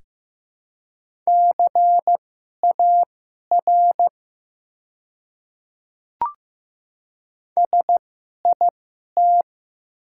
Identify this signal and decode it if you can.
Morse code, heard as a keyed tone, about 15 wpm, spaced as for 11 wpm, 700 Hz. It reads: CAR SIT
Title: 2x Character Spacing - Top 500 Words - 15wpm